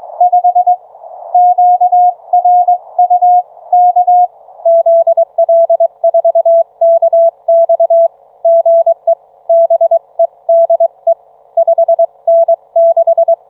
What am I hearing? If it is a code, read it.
Morse code, heard as a keyed tone, about 22 wpm, 660 Hz. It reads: 5 QRU K ZL4KX GE BEDE 5N6